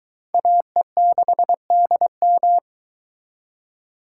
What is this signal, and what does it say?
Morse code, heard as a keyed tone, about 21 wpm, 700 Hz. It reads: AE6DM